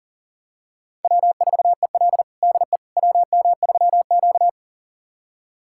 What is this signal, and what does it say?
Morse code, heard as a keyed tone, about 40 wpm, 700 Hz. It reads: W4EL DE WM3Q
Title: USA Callsign DE Callsign - 4 Characters - 40wpm